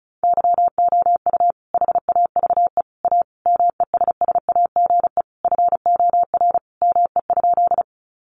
Morse code, heard as a keyed tone, about 34 wpm, 700 Hz. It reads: YOU HAVE A MESSAGE FOR ME?
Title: Sentences from Top 1100 Words 35wpm